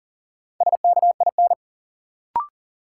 Morse code, heard as a keyed tone, about 40 wpm, 700 Hz. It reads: SKIN
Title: Sentences from Top 1300 Words - New Word Drill Alt - 40wpm